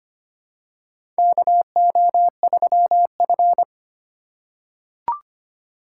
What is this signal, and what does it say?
Morse code, heard as a keyed tone, about 25 wpm, 700 Hz. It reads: KO3F
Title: USA Call Signs - 4 Characters - 2x Character Spacing - 25wpm